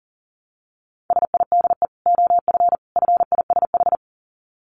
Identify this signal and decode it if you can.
Morse code, heard as a keyed tone, about 40 wpm, 700 Hz. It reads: SIDE OF FISH
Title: Dit Challenge 40wpm